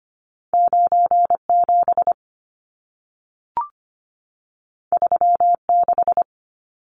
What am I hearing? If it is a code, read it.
Morse code, heard as a keyed tone, about 25 wpm, 700 Hz. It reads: 97 36
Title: Numbers - 2-Digit 25wpm